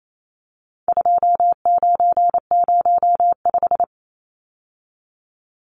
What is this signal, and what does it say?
Morse code, heard as a keyed tone, about 28 wpm, 700 Hz. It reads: 2905
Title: Numbers - 4-Digit 28wpm